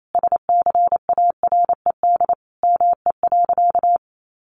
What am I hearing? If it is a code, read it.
Morse code, heard as a keyed tone, about 28 wpm, 700 Hz. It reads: SCARED ME.
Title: Sentences from Top 700 Words 28wpm